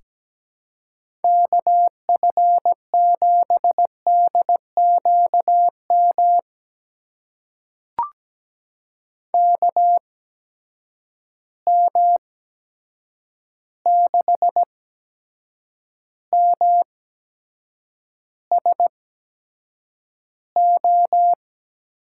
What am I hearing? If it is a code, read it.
Morse code, heard as a keyed tone, about 17 wpm, spaced as for 7 wpm, 700 Hz. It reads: KF7DQM KM6MSO
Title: USA Call Signs - 6 Characters - 8x Character Spacing - 17wpm